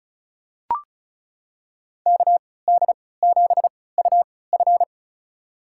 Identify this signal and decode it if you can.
Morse code, heard as a keyed tone, about 36 wpm, spaced as for 20 wpm, 700 Hz. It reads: KD7UF
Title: USA Call Signs - 5 Characters - 3x Character Spacing - 35wpm